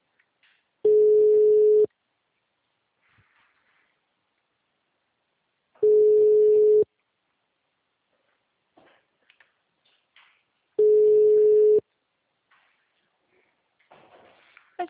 Ringback tone (Europe, 425 Hz).
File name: out- -2003- - - .48